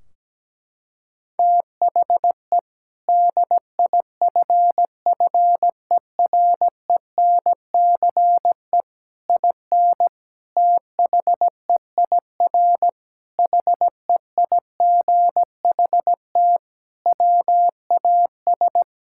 Morse code, heard as a keyed tone, about 17 wpm, 700 Hz. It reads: THE DIFFERENCE IN THEIR HEIGHT WAS
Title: Sentences from Top 1600 Words 17wpm